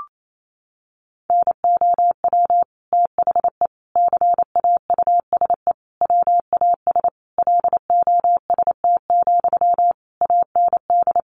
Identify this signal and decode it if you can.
Morse code, heard as a keyed tone, about 28 wpm, 700 Hz. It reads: NOW THE CAUSE WAS LOST, AND
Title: Sentences from Top 1500 Words 28wpm